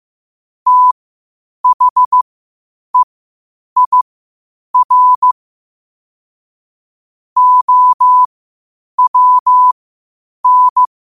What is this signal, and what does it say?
Morse code, heard as a keyed tone, about 15 wpm, spaced as for 8 wpm, 1000 Hz. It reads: THEIR OWN
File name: qotdfiles / 8wpm